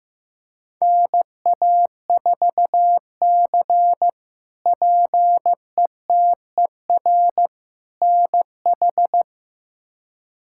Morse code, heard as a keyed tone, about 15 wpm, 700 Hz. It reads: NA4C PETER NH
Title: K1USN SST QSO Practice - 15wpm